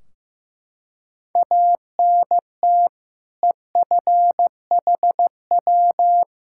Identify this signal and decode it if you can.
Morse code, heard as a keyed tone, about 16 wpm, 700 Hz. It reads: ANT EFHW
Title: Practice QSO - Exchange 2 - Info - 15wpm